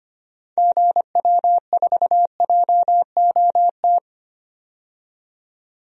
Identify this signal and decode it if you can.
Morse code, heard as a keyed tone, about 25 wpm, 700 Hz. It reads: GW4JOT
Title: DXCC-100 Call Signs - 6 Characters - 25wpm